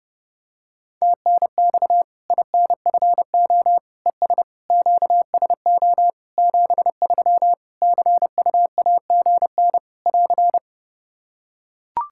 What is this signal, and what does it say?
Morse code, heard as a keyed tone, about 30 wpm, 700 Hz. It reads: TNX INFO ES QSO 73 CUAGN <AR>
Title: Practice QSO - Exchange 4 - Ending - 30wpm